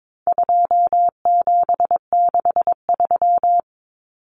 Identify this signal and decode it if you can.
Morse code, heard as a keyed tone, about 22 wpm, 700 Hz. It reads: 2763
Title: Common Cut Numbers - 4-Digit 22wpm